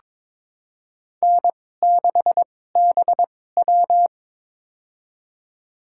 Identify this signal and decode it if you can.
Morse code, heard as a keyed tone, about 21 wpm, spaced as for 16 wpm, 700 Hz. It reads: N6BW
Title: USA Call Signs - 4 Characters - 2x Character Spacing - 22wpm